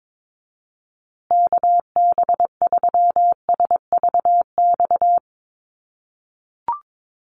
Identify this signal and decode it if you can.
Morse code, heard as a keyed tone, about 22 wpm, 700 Hz. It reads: KB3SVX
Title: USA Call Signs Repeated 2x 22wpm